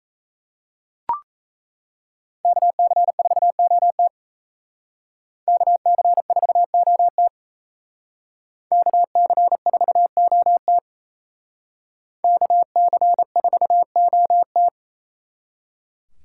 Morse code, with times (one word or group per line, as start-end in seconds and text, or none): 2.5-4.1 KC4OT
5.5-7.3 KC4OT
8.7-10.8 KC4OT
12.2-14.7 KC4OT